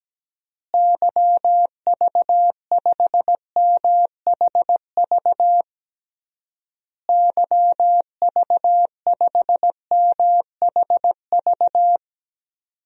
Morse code, heard as a keyed tone, about 17 wpm, 700 Hz. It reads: YV5MHV YV5MHV
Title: DXCC-100 Call Signs - 6 Characters - Repeated 3x - 17wpm